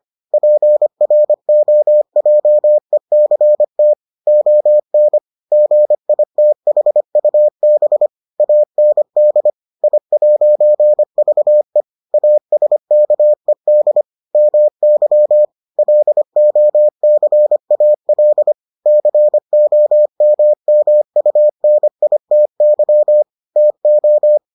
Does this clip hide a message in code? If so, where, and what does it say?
Yes: Morse code from 0.3 to 24.4 s, heard as a keyed tone, about 25 wpm, 600 Hz. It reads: PROJECT ON GITHUB AND I'VE ASKED MY LOCAL COMMUNITY TO